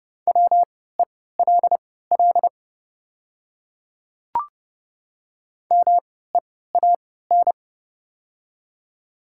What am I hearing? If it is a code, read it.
Morse code, heard as a keyed tone, about 29 wpm, spaced as for 17 wpm, 690 Hz. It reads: WELL MEAN
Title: 3x Character Spacing - Top 500 Words - 30wpm